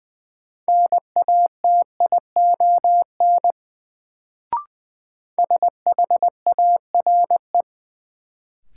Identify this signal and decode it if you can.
Morse code, heard as a keyed tone, about 21 wpm, 700 Hz. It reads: NATION SHARE
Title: Sentences from Top 900 Words - New Word Drill - 20wpm